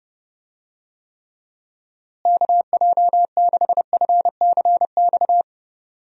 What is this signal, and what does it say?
Morse code, heard as a keyed tone, about 30 wpm, 700 Hz. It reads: KJ6FCX
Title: USA Call Signs - 6 Characters - Repeated 3x - 30wpm